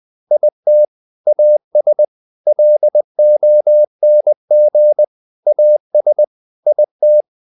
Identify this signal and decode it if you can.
Morse code, heard as a keyed tone, about 20 wpm, 600 Hz. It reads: IT AS LONG AS IT
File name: the old man and the sea part 2 20wpm